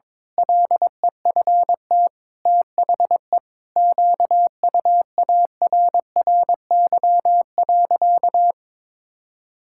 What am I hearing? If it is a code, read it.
Morse code, heard as a keyed tone, about 22 wpm, 700 Hz. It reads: LEFT THE QUARRY.
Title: Pangram Challenge v2 - 22wpm